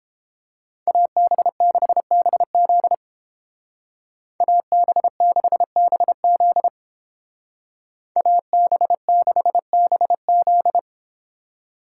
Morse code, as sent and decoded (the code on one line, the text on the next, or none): .- -... -.... -... --.. / .- -... -.... -... --.. / .- -... -.... -... --..
AB6BZ AB6BZ AB6BZ